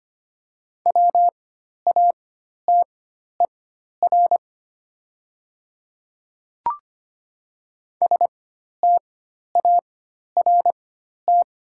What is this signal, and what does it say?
Morse code, heard as a keyed tone, about 25 wpm, spaced as for 12 wpm, 690 Hz. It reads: WATER START
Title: 4x Character Spacing - Top 500 Words - 25wpm